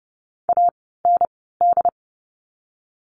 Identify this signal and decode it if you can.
Morse code, heard as a keyed tone, about 30 wpm, 700 Hz. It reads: A N D
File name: Lesson 16f - Letter c - Words 30wpm